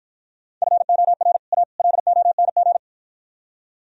Morse code, heard as a keyed tone, about 51 wpm, 700 Hz. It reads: FOR A LONG